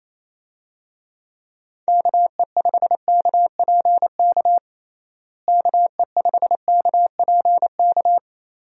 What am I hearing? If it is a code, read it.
Morse code, heard as a keyed tone, about 28 wpm, 700 Hz. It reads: KE5KPK KE5KPK
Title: USA Call Signs - 6 Characters - Repeated 2x - 28wpm